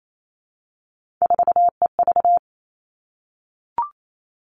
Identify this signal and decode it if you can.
Morse code, heard as a keyed tone, about 28 wpm, 700 Hz. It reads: 4EV
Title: All Cut Numbers - 3-Digit 28wpm